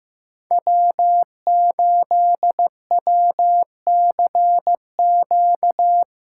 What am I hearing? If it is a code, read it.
Morse code, heard as a keyed tone, about 15 wpm, 700 Hz. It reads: W8WCQ
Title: USA Call Signs - 5 Characters - 8x Character Spacing - 15wpm